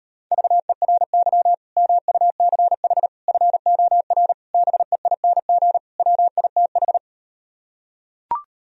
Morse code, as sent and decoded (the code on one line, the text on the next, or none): ...- . .-. -.-- / -- ..- -.-. .... / ..-. --- .-. / -... .. -. --. / .-- .. - ....
VERY MUCH FOR BEING WITH